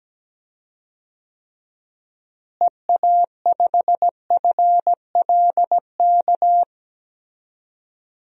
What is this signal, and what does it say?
Morse code, heard as a keyed tone, about 17 wpm, 700 Hz. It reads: EA5FLK
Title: DXCC-100 Call Signs - 6 Characters - Repeated 3x - 17wpm